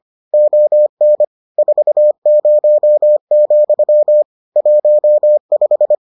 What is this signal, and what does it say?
Morse code, heard as a keyed tone, about 25 wpm, 600 Hz. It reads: ON 40, 15